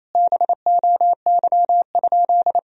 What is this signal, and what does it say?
Morse code, heard as a keyed tone, about 28 wpm, 700 Hz. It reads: BOY?
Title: Sentences from Top 1000 Words - Review Part 1 - 28wpm